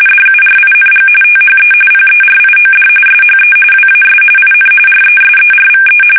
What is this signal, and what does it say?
RTTY, about 45.45 baud. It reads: DURING THE PERIOD SHE WAS IN RESERVE
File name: fsk850 / USS-Massachusetts